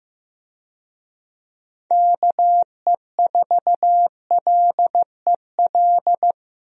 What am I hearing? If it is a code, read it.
Morse code, heard as a keyed tone, about 16 wpm, 700 Hz. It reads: KE4LEL
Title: USA Call Signs - 6 Characters - Repeated 3x - 15wpm